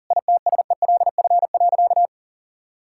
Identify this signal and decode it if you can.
Morse code, heard as a keyed tone, about 40 wpm, 700 Hz. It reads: ITSELF.
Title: Sentences from Top 1000 Words - Review Part 5 - 40wpm